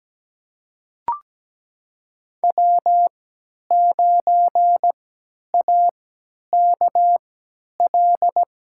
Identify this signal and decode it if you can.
Morse code, heard as a keyed tone, about 17 wpm, spaced as for 10 wpm, 700 Hz. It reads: W9AKL
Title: USA Call Signs - 5 Characters - 3x Character Spacing - 17wpm